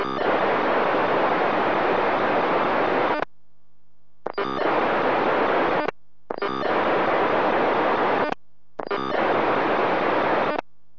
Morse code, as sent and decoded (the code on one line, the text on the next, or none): none